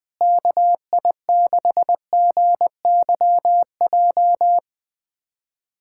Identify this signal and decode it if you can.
Morse code, heard as a keyed tone, about 20 wpm, 700 Hz. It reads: KI6GYJ
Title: USA Callsign DE Callsign 20wpm